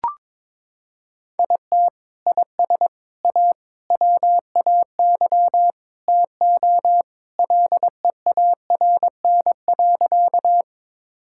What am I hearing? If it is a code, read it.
Morse code, heard as a keyed tone, about 22 wpm, 700 Hz. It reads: IT IS A WAY TO LEARN.
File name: practice / 22wpm / Sentences with 3-Word Phrases - 22wpm